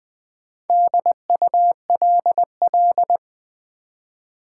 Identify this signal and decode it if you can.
Morse code, heard as a keyed tone, about 20 wpm, 700 Hz. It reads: DULL
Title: Build A Word - 1 Letter At A Time - 4-Letter Words - 20wpm